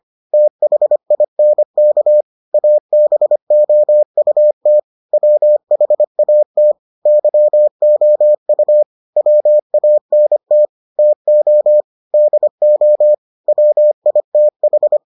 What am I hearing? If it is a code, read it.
Morse code, heard as a keyed tone, about 25 wpm, 600 Hz. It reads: THINK ABOUT WHAT YOU WANT TO DO WITH